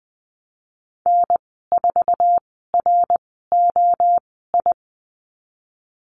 Morse code, heard as a keyed tone, about 20 wpm, spaced as for 15 wpm, 700 Hz. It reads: N4ROI
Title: 069 Lesson 12g - Letter d - Call Signs 20wpm